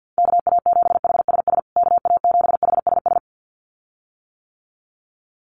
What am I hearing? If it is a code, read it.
Morse code, heard as a keyed tone, about 48 wpm, 700 Hz. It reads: KA7HSS KA7HSS
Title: USA Call Signs Repeated 2x 50wpm